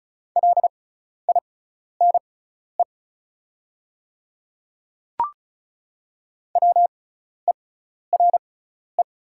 Morse code, heard as a keyed tone, about 36 wpm, spaced as for 12 wpm, 690 Hz. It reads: LINE WERE
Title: 6x Character Spacing - Top 200 Words - 35wpm